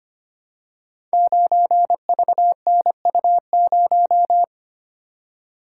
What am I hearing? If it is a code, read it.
Morse code, heard as a keyed tone, about 25 wpm, 700 Hz. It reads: 9VNU0
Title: Code Groups - 5 Letter:Numbers - 25wpm